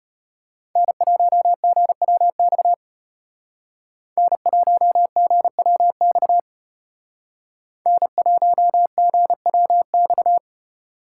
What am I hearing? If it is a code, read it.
Morse code, heard as a keyed tone, about 34 wpm, 700 Hz. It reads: N1GWX N1GWX N1GWX